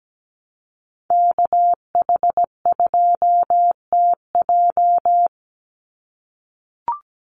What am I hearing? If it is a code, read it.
Morse code, heard as a keyed tone, about 17 wpm, 700 Hz. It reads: KH2TJ
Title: NA SOTA Activator Callsigns 17wpm